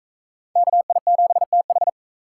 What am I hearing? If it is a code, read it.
Morse code, heard as a keyed tone, about 40 wpm, 700 Hz. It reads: KI7TH